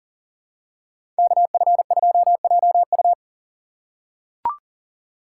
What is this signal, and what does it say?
Morse code, heard as a keyed tone, about 40 wpm, 700 Hz. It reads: KF2JU
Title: USA Call Signs - 5 Characters - 3x Character Spacing - 40wpm